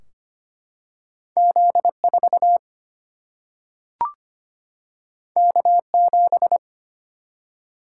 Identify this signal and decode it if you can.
Morse code, heard as a keyed tone, about 25 wpm, 700 Hz. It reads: Z 4 K 7
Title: Single-Letter Single-Number 25wpm